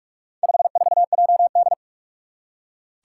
Morse code, heard as a keyed tone, about 44 wpm, 700 Hz. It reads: 54JD
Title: Four Challenging Letter-Number Combinations 45wpm